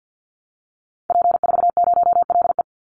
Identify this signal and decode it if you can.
Morse code, heard as a keyed tone, about 50 wpm, 700 Hz. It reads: P40LE
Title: Call Signs Repeated 3x - 50wpm